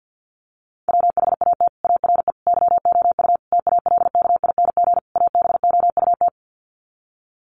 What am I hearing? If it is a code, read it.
Morse code, heard as a keyed tone, about 50 wpm, 700 Hz. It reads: WHAT ARE YOU TALKING ABOUT
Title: Sets of 5 Words 50wpm